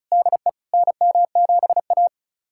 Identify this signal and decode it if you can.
Morse code, heard as a keyed tone, about 34 wpm, 700 Hz. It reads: DE NM7A
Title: USA Callsign DE Callsign - 4 Characters - 35wpm